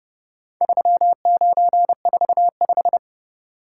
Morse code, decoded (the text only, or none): 3945